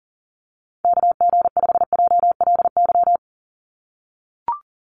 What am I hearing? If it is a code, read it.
Morse code, heard as a keyed tone, about 40 wpm, 700 Hz. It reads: KG5JLY
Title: USA Call Signs 40wpm